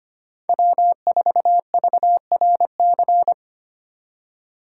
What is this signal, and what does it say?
Morse code, heard as a keyed tone, about 33 wpm, 700 Hz. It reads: W4VRC